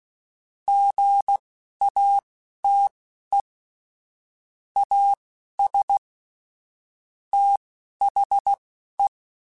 Morse code, heard as a keyed tone, about 16 wpm, spaced as for 11 wpm, 800 Hz. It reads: GATE AS THE
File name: Technical Subjects / Technical Subjects-mp3s / OR Gate @10